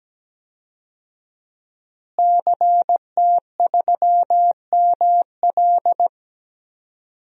Morse code, heard as a keyed tone, about 17 wpm, 700 Hz. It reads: CT3ML
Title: DXCC-100 Call Signs - 5 Characters - Repeated 3x - 17wpm